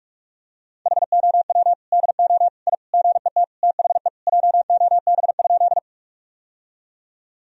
Morse code, heard as a keyed tone, about 46 wpm, 700 Hz. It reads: HOW DO I GET THE JOB?
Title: Sentences from Top 1000 Words - Review Part 3 - 45wpm